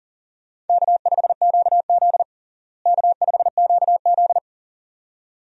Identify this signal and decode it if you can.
Morse code, heard as a keyed tone, about 40 wpm, 700 Hz. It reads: K5QZ K5QZ
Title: USA Call Signs - 4 Characters - Repeated 2x - 40wpm